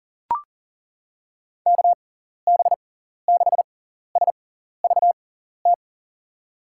Morse code, heard as a keyed tone, about 37 wpm, spaced as for 14 wpm, 700 Hz. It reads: KB6SVT
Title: USA Call Signs - 6 Characters - 6x Character Spacing - 40wpm